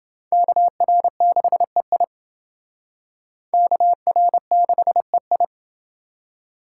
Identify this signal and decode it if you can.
Morse code, heard as a keyed tone, about 28 wpm, 700 Hz. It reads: KR6EI KR6EI